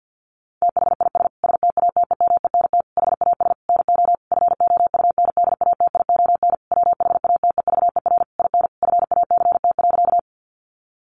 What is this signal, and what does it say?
Morse code, heard as a keyed tone, about 50 wpm, 700 Hz. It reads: THIS STATEMENT HAS NO FOUNDATION WHATEVER IN FACT.
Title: Sentences from Top 1200 Words 50wpm